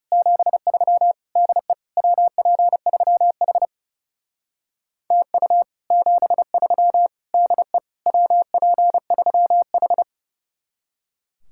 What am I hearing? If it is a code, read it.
Morse code, heard as a keyed tone, about 37 wpm, 700 Hz. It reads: 73 DE WP3H TU 73 DE WP3H
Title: Speed-Racing Common QSO Phrases - 30wpm